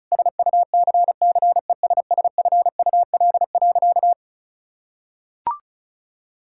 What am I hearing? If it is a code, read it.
Morse code, heard as a keyed tone, about 34 wpm, 700 Hz. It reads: SUCCESSFUL.
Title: Sentences from Top 1600 Words 35wpm